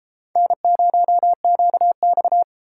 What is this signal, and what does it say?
Morse code, heard as a keyed tone, about 32 wpm, 700 Hz. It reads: N0QX